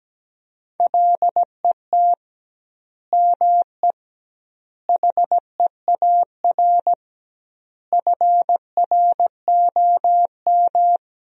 Morse code, heard as a keyed tone, about 17 wpm, 700 Hz. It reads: LET ME HEAR FROM
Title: 2x Word-Spacing - Sentences from Top 400 Words - 17wpm